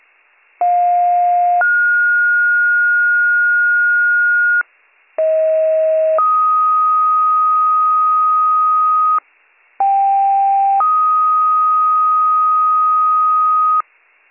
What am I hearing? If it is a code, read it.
Two-tone page, heard as a one-second tone, then a three-second tone, about 710 then 1430 Hz, about 640 then 1250 Hz, about 790 then 1250 Hz. It is heard from 0.6 to 13.8 s.